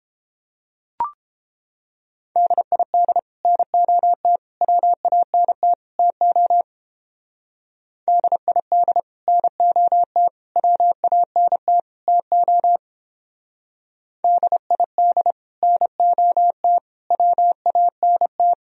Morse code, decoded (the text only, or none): DID NOT WANT TO DID NOT WANT TO DID NOT WANT